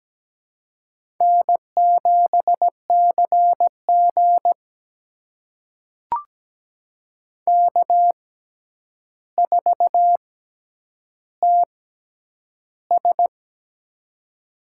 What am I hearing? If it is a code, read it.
Morse code, heard as a keyed tone, about 17 wpm, spaced as for 9 wpm, 700 Hz. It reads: N7CG K4TS
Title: USA Call Signs - 4 Characters - 6x Character Spacing - 17wpm